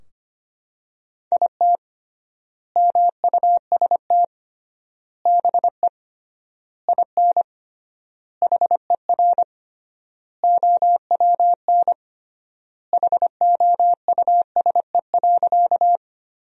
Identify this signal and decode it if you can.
Morse code, heard as a keyed tone, about 25 wpm, 700 Hz. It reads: IT MUST BE IN HER OWN HOUSE.
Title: 3x Word-Spacing - Sentences from Top 300 Words - 25wpm